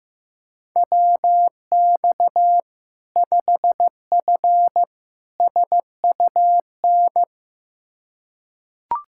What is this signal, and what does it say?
Morse code, heard as a keyed tone, about 15 wpm, 700 Hz. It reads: WX 5F SUN
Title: Practice QSO - Exchange 2 - Info - 15wpm